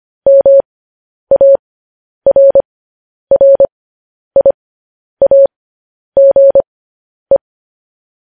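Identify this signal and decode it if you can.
Morse code, heard as a keyed tone, about 25 wpm, spaced as for 10 wpm, 550 Hz. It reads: MARRIAGE
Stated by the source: Morse code news headlines at 10 wpm.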